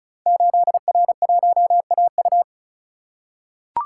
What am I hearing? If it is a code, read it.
Morse code, heard as a keyed tone, about 36 wpm, 690 Hz. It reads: 8R1AU
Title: Code Groups - 5 Letter:Numbers - 35wpm